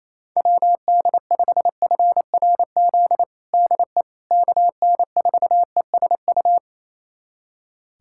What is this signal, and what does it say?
Morse code, heard as a keyed tone, about 27 wpm, 700 Hz. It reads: WD5FRZ DE KN4ESU
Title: USA Callsign DE Callsign 28wpm